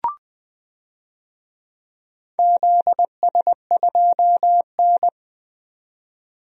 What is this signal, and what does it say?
Morse code, heard as a keyed tone, about 20 wpm, 700 Hz. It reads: ZS2N
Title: DXCC-100 Call Signs - 4 Characters - 20wpm